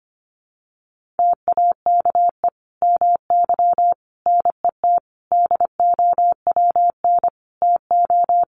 Morse code, heard as a keyed tone, about 25 wpm, 700 Hz. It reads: TAKE MY NET DOWN TO
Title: Sentences from Top 1400 Words 25wpm